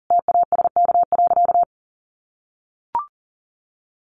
Morse code, heard as a keyed tone, about 40 wpm, 700 Hz. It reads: TASK.
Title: Sentences from Top 1100 Words 40wpm